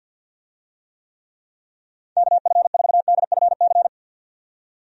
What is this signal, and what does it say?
Morse code, heard as a keyed tone, about 51 wpm, 700 Hz. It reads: KF4DFC
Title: USA Call Signs - 6 Characters - 50wpm